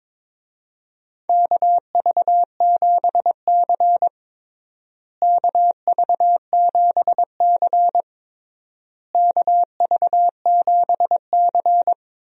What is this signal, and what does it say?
Morse code, heard as a keyed tone, about 22 wpm, 700 Hz. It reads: KV7C KV7C KV7C
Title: USA Call Signs - 4 Characters - Repeated 3x - 22wpm